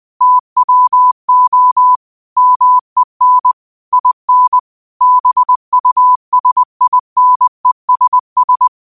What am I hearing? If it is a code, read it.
Morse code, heard as a keyed tone, about 20 wpm, 1000 Hz. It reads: TWO MEN IN BUSINESS
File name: qotdfiles / 20wpm